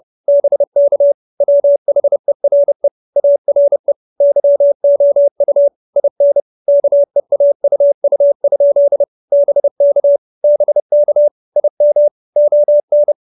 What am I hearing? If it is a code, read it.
Morse code, heard as a keyed tone, about 30 wpm, 570 Hz. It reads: BK WHERE ARE YOU IN KEAUU? BK BK IM ON
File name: QSO311 30